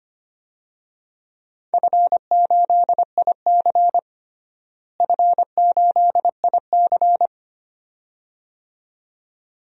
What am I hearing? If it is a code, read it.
Morse code, heard as a keyed tone, about 25 wpm, 700 Hz. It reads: F8IC F8IC
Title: DXCC-100 Call Signs - 4 Characters - Repeated 2x - 25wpm